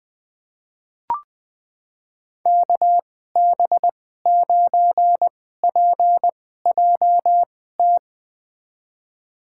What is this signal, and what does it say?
Morse code, heard as a keyed tone, about 21 wpm, spaced as for 15 wpm, 700 Hz. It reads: KB9PJT
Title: USA Call Signs - 6 Characters - 2x Character Spacing - 20wpm